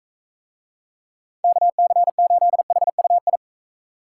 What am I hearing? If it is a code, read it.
Morse code, heard as a keyed tone, about 41 wpm, 700 Hz. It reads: KC8HUI